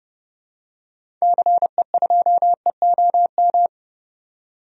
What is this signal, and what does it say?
Morse code, heard as a keyed tone, about 30 wpm, 700 Hz. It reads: CE2EOM
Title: DXCC-100 Call Signs - 6 Characters - 30wpm